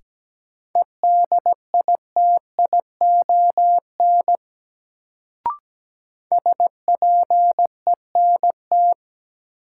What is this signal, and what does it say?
Morse code, heard as a keyed tone, about 17 wpm, 700 Hz. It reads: EDITION SPENT
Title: Sets of 5 Words - New Word Drill - 17wpm